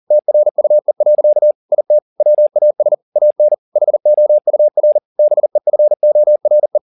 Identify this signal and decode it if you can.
Morse code, heard as a keyed tone, about 40 wpm, 600 Hz. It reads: TRUE. IT WAS AN HOUR BEFORE
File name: the old man and the sea 2 40wpm